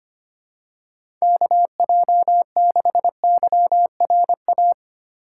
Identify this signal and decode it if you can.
Morse code, heard as a keyed tone, about 25 wpm, 700 Hz. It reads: KJ6YRA
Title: USA Call Signs - 6 Characters - 3x Character Spacing - 25wpm